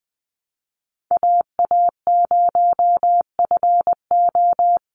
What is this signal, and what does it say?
Morse code, heard as a keyed tone, about 20 wpm, 700 Hz. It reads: AA0FO
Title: Call Signs Repeated 3x - 20wpm